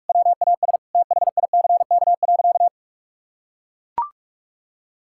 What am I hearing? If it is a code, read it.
Morse code, heard as a keyed tone, about 42 wpm, 700 Hz. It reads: WAS THICK.
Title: Sentences from Top 1600 Words 45wpm